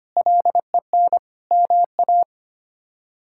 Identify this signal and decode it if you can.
Morse code, heard as a keyed tone, about 25 wpm, 700 Hz. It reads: LEN MA
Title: K1USN SST QSO Practice - 25wpm